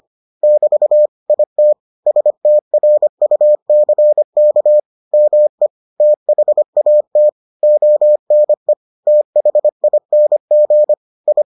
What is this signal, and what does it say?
Morse code, heard as a keyed tone, about 25 wpm, 600 Hz. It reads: <BT> IT STRUCK ME THAT ONE THING I